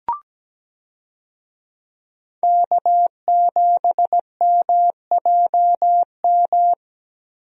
Morse code, heard as a keyed tone, about 17 wpm, 700 Hz. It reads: K7MJM
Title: USA Call Signs - 5 Characters - Repeated 3x - 17wpm